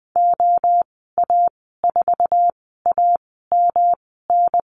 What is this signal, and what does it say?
Morse code, heard as a keyed tone, about 20 wpm, spaced as for 15 wpm, 700 Hz. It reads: OA4AMN
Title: 147 Lesson 24g - Letter y - Call Signs 20wpm